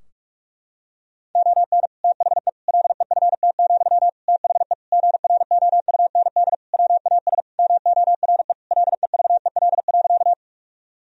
Morse code, heard as a keyed tone, about 46 wpm, 700 Hz. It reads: ON THE LEFT, THE GROUND WAS MORE LEVEL.
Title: Sentences from Top 1000 Words - Review Part 4 - 45wpm